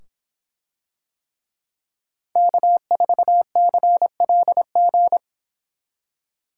Morse code, heard as a keyed tone, about 24 wpm, 700 Hz. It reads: K4CLG